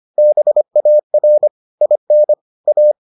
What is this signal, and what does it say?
Morse code, heard as a keyed tone, about 25 wpm, 600 Hz. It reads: BAR IN A